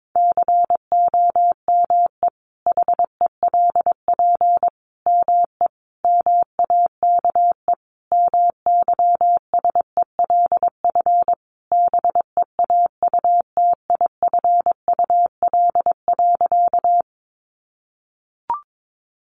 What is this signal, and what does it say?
Morse code, heard as a keyed tone, about 22 wpm, 700 Hz. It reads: COME HELP ME MAKE MYSELF BEAUTIFUL.
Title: Sentences from Top 1300 Words 22wpm